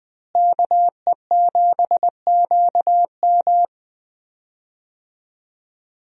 Morse code, heard as a keyed tone, about 20 wpm, 700 Hz. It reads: KE7QM
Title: USA Call Signs - 5 Characters - Repeated 2x - 20wpm